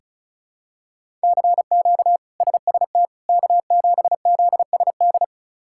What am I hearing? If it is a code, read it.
Morse code, heard as a keyed tone, about 34 wpm, 700 Hz. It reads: CQ SST K7ZSD
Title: K1USN SST QSO Practice - 35wpm